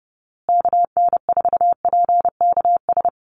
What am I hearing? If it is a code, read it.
Morse code, heard as a keyed tone, about 30 wpm, 700 Hz. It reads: KN4PKS
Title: USA Call Signs Repeated 2x 30wpm